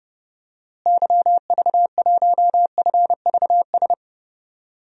Morse code, heard as a keyed tone, about 30 wpm, 700 Hz. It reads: YV1FVS
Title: DXCC-100 Call Signs - 6 Characters - Repeated 2x - 30wpm